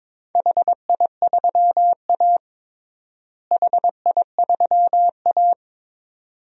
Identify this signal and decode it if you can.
Morse code, heard as a keyed tone, about 22 wpm, 700 Hz. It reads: HI3A HI3A
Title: DXCC-100 Call Signs - 4 Characters - Repeated 2x - 22wpm